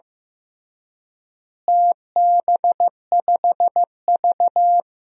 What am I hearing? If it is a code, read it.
Morse code, heard as a keyed tone, about 15 wpm, 700 Hz. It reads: TB5V
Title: DXCC-100 Call Signs - 4 Characters - Repeated 2x - 15wpm